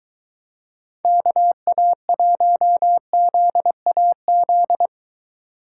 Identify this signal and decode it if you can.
Morse code, heard as a keyed tone, about 22 wpm, 700 Hz. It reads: KA1ZAZ